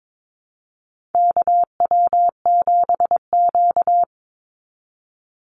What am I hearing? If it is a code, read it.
Morse code, heard as a keyed tone, about 22 wpm, 700 Hz. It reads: KW7Q
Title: Call Signs Repeated 2x - 22wpm